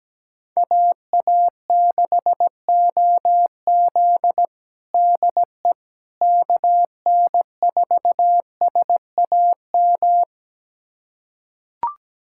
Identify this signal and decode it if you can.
Morse code, heard as a keyed tone, about 17 wpm, 700 Hz. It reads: AA6OZ DE KN4SAM
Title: USA Callsign DE Callsign 17wpm